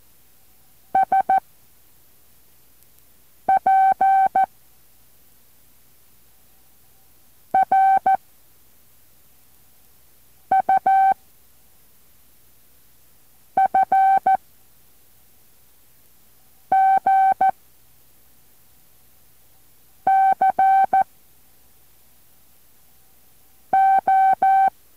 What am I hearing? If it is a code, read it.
Morse code, heard as a keyed tone, about 14 wpm, 770 Hz. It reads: S P R U F G C O
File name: intl morse code2